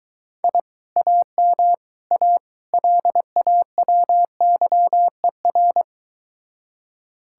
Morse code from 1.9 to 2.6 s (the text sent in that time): A